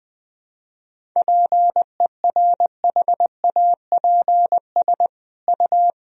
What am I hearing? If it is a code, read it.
Morse code, heard as a keyed tone, about 21 wpm, 700 Hz. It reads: PERHAPS U
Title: Sentences from Top 500 Words and Abbreviations 20wpm